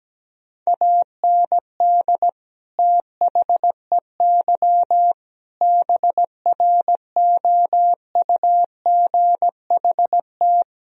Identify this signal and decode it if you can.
Morse code, heard as a keyed tone, about 17 wpm, 700 Hz. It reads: AND THEY BROUGHT
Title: Sentences from Top 1000 Words - Review Part 2 - 17wpm